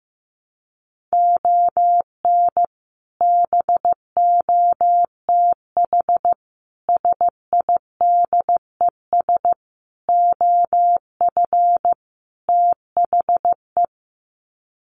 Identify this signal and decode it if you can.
Morse code, heard as a keyed tone, about 15 wpm, 700 Hz. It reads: ON BOTH SIDES OF THE
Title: Sets of 5 Words 15wpm